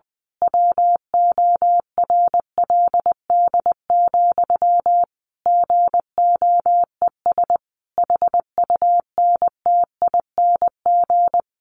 Morse code, heard as a keyed tone, about 20 wpm, 700 Hz. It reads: WORLD, GOES HUNTING
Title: A Sound Of Thunder 20wpm